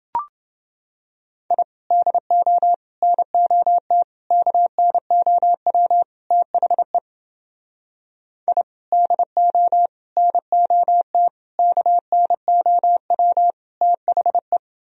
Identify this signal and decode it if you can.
Morse code, heard as a keyed tone, about 29 wpm, 700 Hz. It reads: I DO NOT KNOW THE I DO NOT KNOW THE